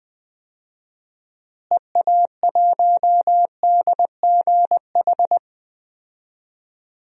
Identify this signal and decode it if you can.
Morse code, heard as a keyed tone, about 21 wpm, 700 Hz. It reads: EA1DGH
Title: DXCC-100 Call Signs - 6 Characters - Repeated 3x - 20wpm